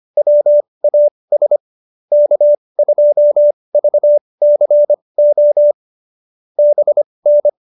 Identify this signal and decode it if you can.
Morse code, heard as a keyed tone, about 25 wpm, spaced as for 20 wpm, 590 Hz. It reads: WAS K2VCO BN